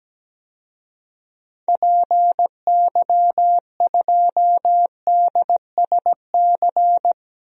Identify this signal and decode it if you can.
Morse code, heard as a keyed tone, about 17 wpm, 700 Hz. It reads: PY2DSC
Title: DXCC-100 Call Signs - 6 Characters - Repeated 2x - 17wpm